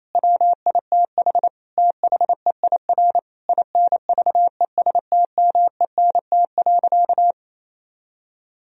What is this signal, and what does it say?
Morse code, heard as a keyed tone, about 28 wpm, 700 Hz. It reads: WITH THEIR INVESTMENT.
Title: Sentences from Top 1000 Words - Review Part 4 - 28wpm